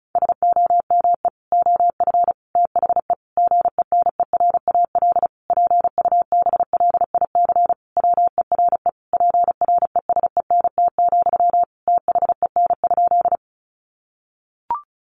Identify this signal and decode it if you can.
Morse code, heard as a keyed tone, about 34 wpm, 700 Hz. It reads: SOME OF THE GENERAL PUBLIC WERE PRESENT, THEN?
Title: Sentences from Top 800 Words 35wpm